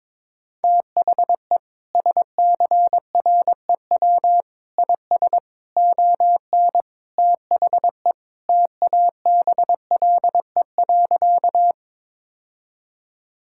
Morse code, heard as a keyed tone, about 22 wpm, 700 Hz. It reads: THE SCREW IS ON THE TABLE.
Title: Sentences from Top 1600 Words 22wpm